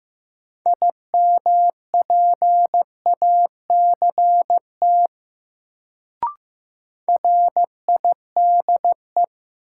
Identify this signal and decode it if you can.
Morse code, heard as a keyed tone, about 15 wpm, 700 Hz. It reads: IMPACT RIDE
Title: Sentences from Top 1000 Words - Review Part 6 - New Word Drill - 15wpm